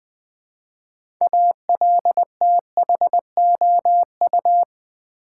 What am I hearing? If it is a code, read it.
Morse code, heard as a keyed tone, about 20 wpm, 700 Hz. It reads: ALTHOU
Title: Build A Word - 2 Letters At A Time - 8-Letter Words - 20wpm